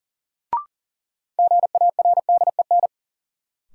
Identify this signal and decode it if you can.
Morse code, heard as a keyed tone, about 38 wpm, 700 Hz. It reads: GARDEN
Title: Sentences from Top 700 Words - New Word Drill - 40wpm